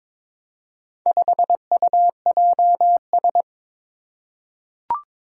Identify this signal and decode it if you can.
Morse code, heard as a keyed tone, about 22 wpm, 700 Hz. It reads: 5UJS
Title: Four Challenging Letter-Number Combinations 22wpm